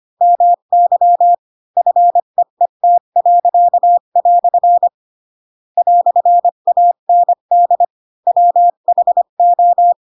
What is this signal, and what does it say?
Morse code, heard as a keyed tone, about 25 wpm, 700 Hz. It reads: MY FEET." "AND WHO
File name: Pinocchio / Pinocchio ch7 20 25 ik1hge